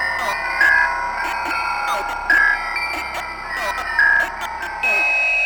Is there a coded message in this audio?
Yes: Emergency Alert System (SAME) tones from 0.4 to 4.3 s, heard as harsh buzzing bursts, with speech.